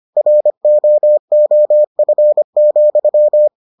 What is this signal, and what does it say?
Morse code, heard as a keyed tone, about 25 wpm, 600 Hz. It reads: ROOF,